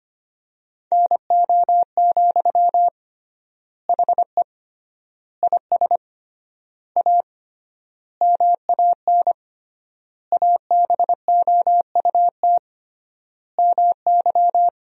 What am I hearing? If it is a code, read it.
Morse code, heard as a keyed tone, about 25 wpm, 700 Hz. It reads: NO, HE IS A MAN ABOUT MY
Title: 3x Word-Spacing - Sentences from Top 400 Words - 25wpm